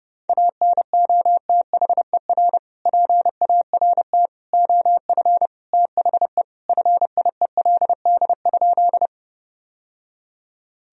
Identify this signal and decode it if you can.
Morse code, heard as a keyed tone, about 30 wpm, 700 Hz. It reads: ANOTHER PART OF THE FIELD?
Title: Sentences from Top 1000 Words - Review Part 2 - 30wpm